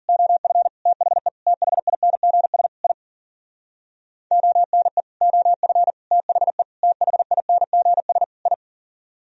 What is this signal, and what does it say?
Morse code, heard as a keyed tone, about 46 wpm, 700 Hz. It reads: OF THE THINGS I ONE OF THE THINGS I
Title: Speed-Racing - Sets of 5 Words 40wpm